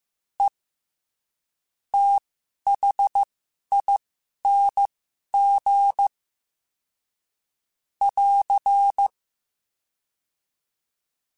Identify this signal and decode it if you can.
Morse code, heard as a keyed tone, about 15 wpm, spaced as for 11 wpm, 800 Hz. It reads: E THING <AR>
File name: Special INGs-mp3s / Special INGs-mp3s / SlowToFaster ing @10